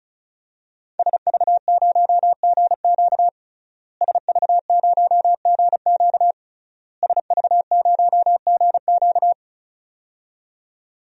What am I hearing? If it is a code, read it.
Morse code, heard as a keyed tone, about 36 wpm, 700 Hz. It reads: SV0GQ SV0GQ SV0GQ
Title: DXCC-100 Call Signs - 5 Characters - Repeated 3x - 35wpm